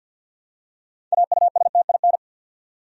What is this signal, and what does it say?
Morse code, heard as a keyed tone, about 53 wpm, 700 Hz. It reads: AUSTIN
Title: USA Common Names - 50wpm